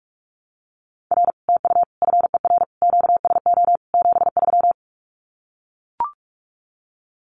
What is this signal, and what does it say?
Morse code, heard as a keyed tone, about 44 wpm, 700 Hz. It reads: R TU FER QSO 73
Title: Common QSO Phrases Encore 45wpm